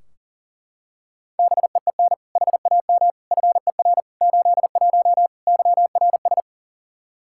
Morse code, heard as a keyed tone, about 37 wpm, 700 Hz. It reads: BEEN HAM FER 81 YRS
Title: Practice QSO - Exchange 3 - More Info - 40wpm